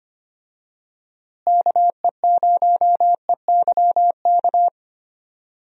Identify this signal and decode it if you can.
Morse code, heard as a keyed tone, about 25 wpm, 700 Hz. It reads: KE0EYK
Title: USA Call Signs - 6 Characters - Repeated 3x - 25wpm